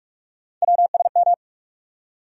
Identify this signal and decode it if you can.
Morse code, heard as a keyed tone, about 46 wpm, 700 Hz. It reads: WSM
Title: USA Call Sign Postfixes - 3 Letters 45wpm